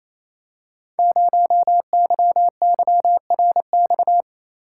Morse code, heard as a keyed tone, about 29 wpm, 700 Hz. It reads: 0YYRX
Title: Code Groups - 5 Letter:Numbers - 28wpm